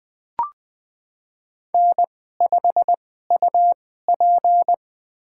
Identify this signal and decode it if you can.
Morse code, heard as a keyed tone, about 21 wpm, spaced as for 15 wpm, 700 Hz. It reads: N5UP
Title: USA Call Signs - 2x Character Spacing - 20wpm